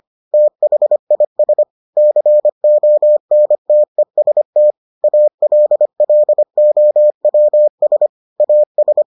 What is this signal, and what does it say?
Morse code, heard as a keyed tone, about 25 wpm, 600 Hz. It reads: THIS CONTEST ALLOWS AS